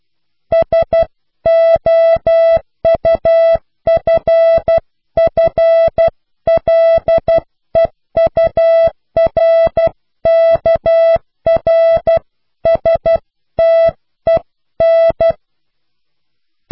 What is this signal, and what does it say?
Morse code, heard as a keyed tone, about 12 wpm, 660 Hz. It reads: SOUFFLEURKRSTEN
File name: cw / Pseudo woorden (15)